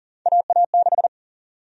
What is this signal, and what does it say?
Morse code, heard as a keyed tone, about 40 wpm, 700 Hz. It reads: AA6
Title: USA Call Sign Prefixes - 40wpm